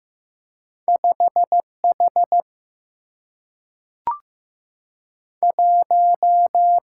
Morse code, heard as a keyed tone, about 15 wpm, 700 Hz. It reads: 5 H 1
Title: Single-Letter Single-Number 15wpm